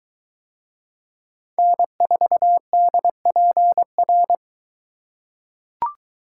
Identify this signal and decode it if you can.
Morse code, heard as a keyed tone, about 22 wpm, 700 Hz. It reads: N4DPR